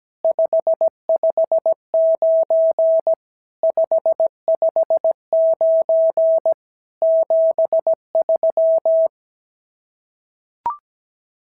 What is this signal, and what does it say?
Morse code, heard as a keyed tone, about 17 wpm, 650 Hz. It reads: 559 559 73
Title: SOTA QSO Practice 17wpm